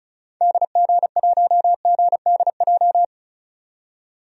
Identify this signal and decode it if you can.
Morse code, heard as a keyed tone, about 36 wpm, 700 Hz. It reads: DG1GDJ
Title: DXCC-100 Call Signs - 6 Characters - 35wpm